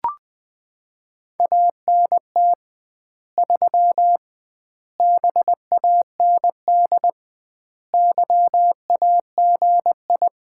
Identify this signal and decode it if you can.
Morse code, heard as a keyed tone, about 21 wpm, 700 Hz. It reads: ANT 3 BAND YAGI
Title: Common QSO Phrases - 2x Word-Spacing - 20wpm